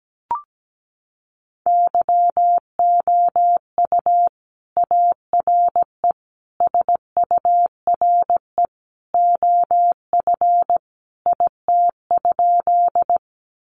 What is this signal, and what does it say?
Morse code, heard as a keyed tone, about 17 wpm, 700 Hz. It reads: YOU ARE SURE OF IT?
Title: Sentences from Top 300 Words 17wpm